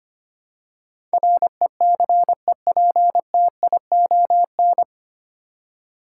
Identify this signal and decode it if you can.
Morse code, heard as a keyed tone, about 25 wpm, 700 Hz. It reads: RECEPTION
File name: Nine Letter Words - 25wpm